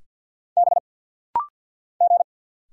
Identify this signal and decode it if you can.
Morse code, heard as a keyed tone, about 48 wpm, 700 Hz. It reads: B G
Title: Single Letter-Number - Rapid-Fire 50wpm